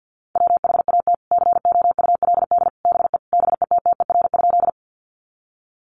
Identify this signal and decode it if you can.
Morse code, heard as a keyed tone, about 50 wpm, 700 Hz. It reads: WHAT COULD BE BETTER?
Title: Sentences from Top 400 Words 50wpm